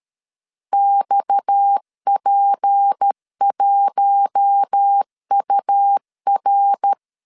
Morse code, heard as a keyed tone, about 13 wpm, 800 Hz. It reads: XP1UR
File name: CW / DET / 1 6 1500C S A 12wpm 3440